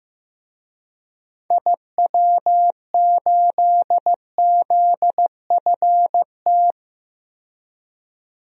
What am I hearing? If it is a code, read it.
Morse code, heard as a keyed tone, about 15 wpm, 700 Hz. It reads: IW8ZFT
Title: DXCC-100 Call Signs - 6 Characters - 15wpm